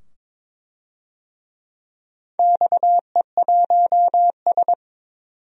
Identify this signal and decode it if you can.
Morse code, heard as a keyed tone, about 22 wpm, 700 Hz. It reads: XE1S
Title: DXCC-100 Call Signs - 4 Characters - 22wpm